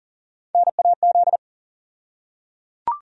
Morse code, heard as a keyed tone, about 39 wpm, 700 Hz. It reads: NAZ
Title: USA Call Sign Postfixes - 3 Letters 40wpm